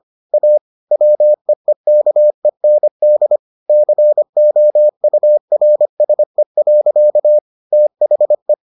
Morse code, heard as a keyed tone, about 25 wpm, 600 Hz. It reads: A WEEKEND COURSE. THE